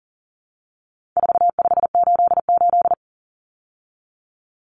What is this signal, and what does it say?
Morse code, heard as a keyed tone, about 40 wpm, 700 Hz. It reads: 4588
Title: Common Cut Numbers - 4-Digit 40wpm